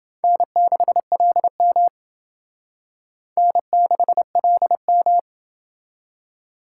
Morse code, heard as a keyed tone, about 28 wpm, 700 Hz. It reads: N6LM N6LM